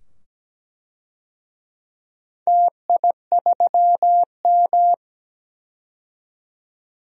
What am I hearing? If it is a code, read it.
Morse code, heard as a keyed tone, about 17 wpm, 700 Hz. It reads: TI3M
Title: DXCC-100 Call Signs - 4 Characters - 17wpm